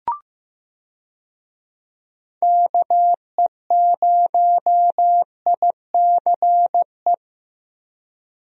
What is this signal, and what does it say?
Morse code, heard as a keyed tone, about 15 wpm, 700 Hz. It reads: KE0ICE
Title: USA Call Signs - 6 Characters - Repeated 3x - 15wpm